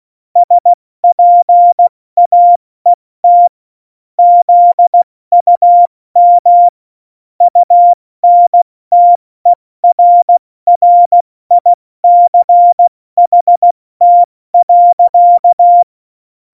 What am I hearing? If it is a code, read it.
Morse code, heard as a keyed tone, about 16 wpm, 700 Hz. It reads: SPAET ZUM UNTERRICHT.
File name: Morsekurs / Witze / Witz 30. April 2019, 70 F80